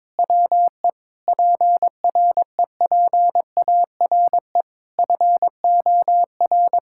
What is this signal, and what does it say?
Morse code, heard as a keyed tone, about 22 wpm, 700 Hz. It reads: WE PREPARE FOR
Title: Sentences from Top 1600 Words 22wpm